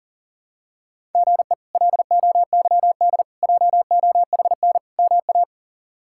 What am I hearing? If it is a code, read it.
Morse code, heard as a keyed tone, about 40 wpm, 700 Hz. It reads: GE LOYD JOHN MA
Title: K1USN SST QSO Practice - 40wpm